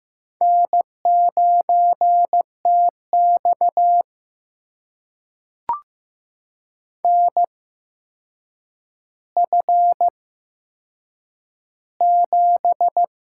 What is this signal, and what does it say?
Morse code, heard as a keyed tone, about 15 wpm, spaced as for 7 wpm, 700 Hz. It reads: N9TX NF7
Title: USA Call Signs - 4 Characters - 8x Character Spacing - 15wpm